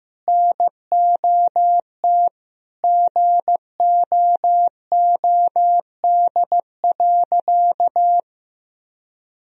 Morse code, heard as a keyed tone, about 15 wpm, 700 Hz. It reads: NOT GOOD.
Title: Sentences from Top 1000 Words - Review Part 3 - 15wpm